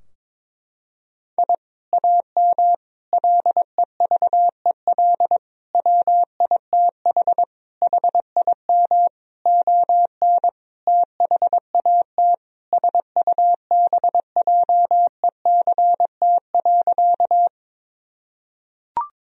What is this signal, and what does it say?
Morse code, heard as a keyed tone, about 22 wpm, 700 Hz. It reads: I AM LEVEL WITH HIM ON THAT SUBJECT.
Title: Sentences from Top 1000 Words - Review Part 4 - 22wpm